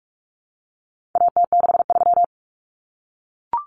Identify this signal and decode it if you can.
Morse code, heard as a keyed tone, about 44 wpm, 700 Hz. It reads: AT63
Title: Common Cut Numbers - 4-Digit 45wpm